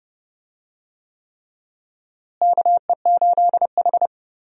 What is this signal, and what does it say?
Morse code, heard as a keyed tone, about 32 wpm, 700 Hz. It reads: KE8H
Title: USA Call Signs - 4 Characters - Repeated 3x - 30wpm